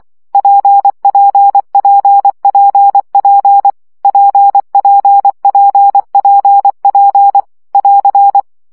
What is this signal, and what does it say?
Morse code, heard as a keyed tone, about 24 wpm, 800 Hz. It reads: PPPPP PPPPP <AR>
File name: CW gyakorlas p 20 baud